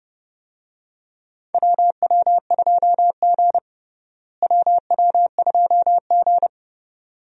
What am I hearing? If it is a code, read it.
Morse code, heard as a keyed tone, about 30 wpm, 700 Hz. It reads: WW2G WW2G
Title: USA Call Signs - 4 Characters - Repeated 2x - 30wpm